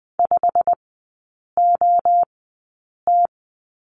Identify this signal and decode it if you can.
Morse code, heard as a keyed tone, about 20 wpm, spaced as for 8 wpm, 700 Hz. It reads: 5OT
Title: 114 Lesson 19e - Letter w - Three Characters 20wpm